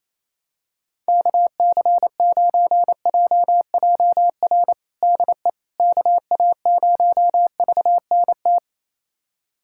Morse code, heard as a keyed tone, about 27 wpm, 700 Hz. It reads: KC9JJR DE KA0VNT
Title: USA Callsign DE Callsign - 6 Characters - 28wpm